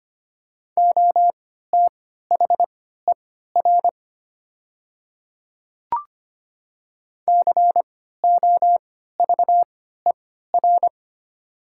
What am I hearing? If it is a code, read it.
Morse code, heard as a keyed tone, about 25 wpm, spaced as for 14 wpm, 700 Hz. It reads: OTHER COVER
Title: 3x Character Spacing - Top 300 Words - 25wpm